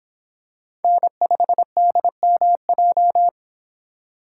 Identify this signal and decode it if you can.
Morse code, heard as a keyed tone, about 25 wpm, 700 Hz. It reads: N5DMJ